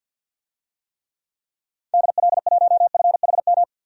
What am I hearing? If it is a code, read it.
Morse code, heard as a keyed tone, about 51 wpm, 700 Hz. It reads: DL1FHM
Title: DXCC-100 Call Signs - 6 Characters - 50wpm